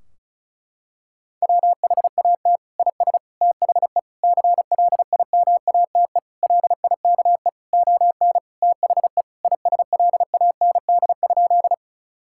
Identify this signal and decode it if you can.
Morse code, heard as a keyed tone, about 36 wpm, 700 Hz. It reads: WHAT IS THE CLIMATE LIKE ON THE ISLAND?
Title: Sentences from Top 1600 Words 35wpm